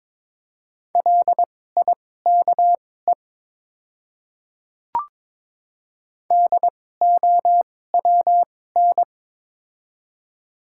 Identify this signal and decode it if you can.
Morse code, heard as a keyed tone, about 21 wpm, spaced as for 16 wpm, 700 Hz. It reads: LIKE DOWN
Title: 2x Character Spacing - Top 100 Words - 22wpm